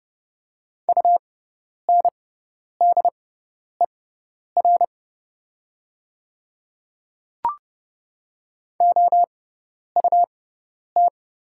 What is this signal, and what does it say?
Morse code, heard as a keyed tone, about 30 wpm, spaced as for 10 wpm, 700 Hz. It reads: UNDER OUT
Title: 6x Character Spacing - Top 200 Words - 30wpm